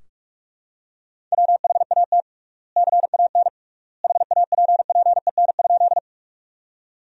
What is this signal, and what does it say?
Morse code, heard as a keyed tone, about 46 wpm, 700 Hz. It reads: WHAT CAN HAPPEN?
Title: 3x Word-Spacing - Sentences from Top 400 Words - 45wpm